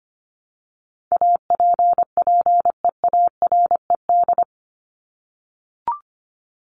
Morse code, heard as a keyed tone, about 25 wpm, 700 Hz. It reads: APPEARED
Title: ED Suffix - Top 100 Words with Base Words 25wpm